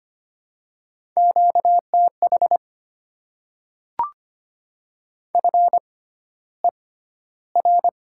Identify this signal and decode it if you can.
Morse code, heard as a keyed tone, about 25 wpm, spaced as for 13 wpm, 700 Hz. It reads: QTH FER
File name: Top 20 QSO Elements - 6x Character Spacing - 25wpm